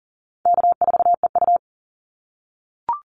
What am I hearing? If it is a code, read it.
Morse code, heard as a keyed tone, about 40 wpm, 700 Hz. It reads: K4EU
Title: Easy Call Signs - 40wpm